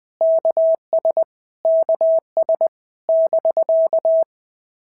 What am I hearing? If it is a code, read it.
Morse code, heard as a keyed tone, about 20 wpm, 650 Hz. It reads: KS KS <BK>
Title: POTA QSO Practice - 20wpm